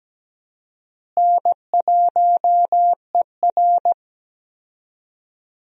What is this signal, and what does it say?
Morse code, heard as a keyed tone, about 17 wpm, 700 Hz. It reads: N1ER
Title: USA Call Signs - 4 Characters - Repeated 3x - 17wpm